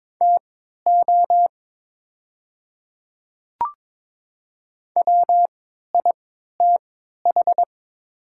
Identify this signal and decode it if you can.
Morse code, heard as a keyed tone, about 22 wpm, spaced as for 13 wpm, 700 Hz. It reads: TO WITH
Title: 3x Character Spacing - Top 100 Words - 22wpm